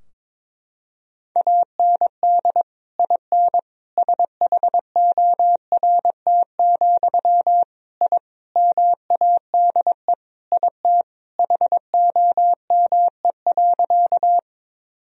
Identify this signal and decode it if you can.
Morse code, heard as a keyed tone, about 22 wpm, 700 Hz. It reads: AND IN SHORT, I MADE IT HOME.
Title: Sentences from Top 1000 Words - Review Part 2 - 22wpm